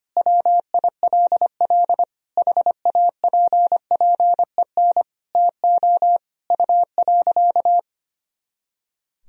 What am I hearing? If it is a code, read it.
Morse code, heard as a keyed tone, about 25 wpm, 700 Hz. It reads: WILL HAPPEN TO U.
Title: Sentences from Top 500 Words and Abbreviations 25wpm